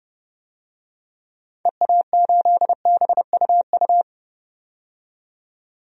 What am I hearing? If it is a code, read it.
Morse code, heard as a keyed tone, about 32 wpm, 700 Hz. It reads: EA8BUU
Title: DXCC-100 Call Signs - 6 Characters - 30wpm